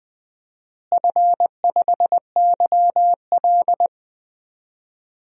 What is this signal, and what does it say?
Morse code, heard as a keyed tone, about 20 wpm, 700 Hz. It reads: F5YL
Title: DXCC-100 Call Signs - 4 Characters - Repeated 2x - 20wpm